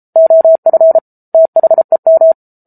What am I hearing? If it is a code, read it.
Morse code, heard as a keyed tone, about 34 wpm, 650 Hz. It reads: OF THEM